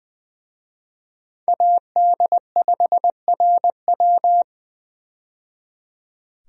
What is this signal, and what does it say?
Morse code, heard as a keyed tone, about 20 wpm, 700 Hz. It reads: AD5RW
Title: USA Call Signs - 5 Characters - 20wpm